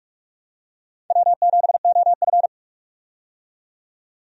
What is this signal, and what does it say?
Morse code, heard as a keyed tone, about 46 wpm, 700 Hz. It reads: W7OF
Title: USA Call Signs - 4 Characters - 45wpm